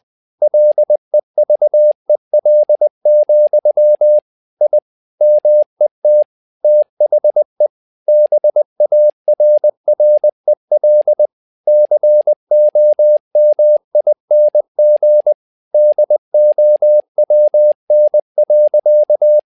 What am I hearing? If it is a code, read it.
Morse code, heard as a keyed tone, about 20 wpm, 600 Hz. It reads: LEVEL, I MET THE BARREL COMING DOWN.
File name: tower trouble 20wpm